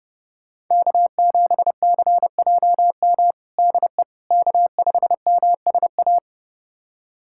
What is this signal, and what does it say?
Morse code, heard as a keyed tone, about 30 wpm, 700 Hz. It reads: K7CJM DE K5MSA
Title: USA Callsign DE Callsign - 5 Characters - 30wpm